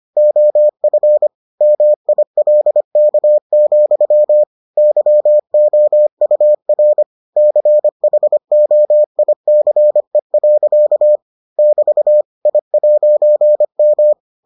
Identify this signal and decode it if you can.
Morse code, heard as a keyed tone, about 25 wpm, 600 Hz. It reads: OF MILK, YOUR CHOICE. <BT> I'M